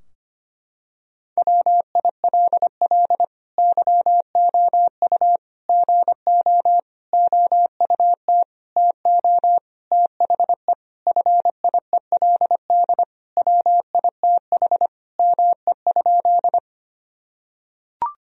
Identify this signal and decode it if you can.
Morse code, heard as a keyed tone, about 25 wpm, 700 Hz. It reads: WILL YOU GO OUT TO THE FIELD WITH ME?
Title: Sentences from Top 1000 Words - Review Part 2 - 25wpm